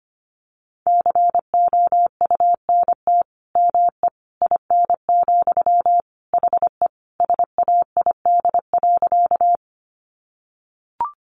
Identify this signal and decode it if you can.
Morse code, heard as a keyed tone, about 25 wpm, 700 Hz. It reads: COUNT ME IN, HE SAID.
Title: Sentences from Top 1500 Words 25wpm